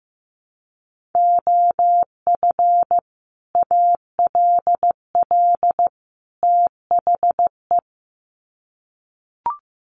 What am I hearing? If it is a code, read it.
Morse code, heard as a keyed tone, about 15 wpm, 700 Hz. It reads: OF ALL THE
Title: Sets of 3 Words 15wpm